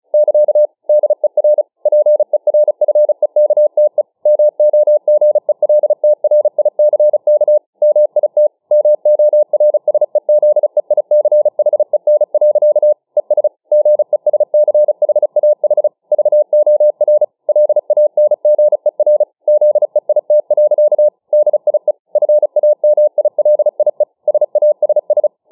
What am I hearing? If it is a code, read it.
Morse code, heard as a keyed tone, about 34 wpm, 600 Hz. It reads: <KA> DER PERFEKTE MOGELTRICK MIT MORSEZEICHEN. ES GESCHAH VOR LANGER ZEIT. DIE FAMILIE SASS